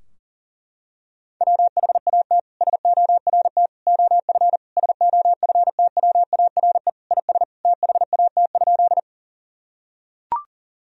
Morse code, heard as a keyed tone, about 40 wpm, 700 Hz. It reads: WHAT SORT OF SOFTWARE IS THAT?
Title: Sentences from Top 1000 Words - Review Part 3 - 40wpm